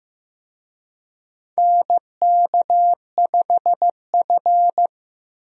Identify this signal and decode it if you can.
Morse code, heard as a keyed tone, about 15 wpm, 700 Hz. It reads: NK5F
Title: USA Call Signs - 4 Characters - Repeated 3x - 15wpm